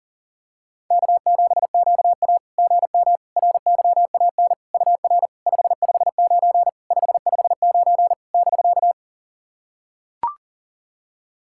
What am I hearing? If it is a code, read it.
Morse code, heard as a keyed tone, about 37 wpm, 690 Hz. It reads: K7QA GM RYAN UR 559 559 <BK>
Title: SOTA QSO Practice 40wpm